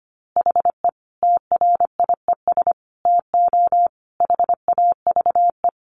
Morse code, heard as a keyed tone, about 25 wpm, 700 Hz. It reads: HE TRIES TO HAVE
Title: Easy Phrases 25wpm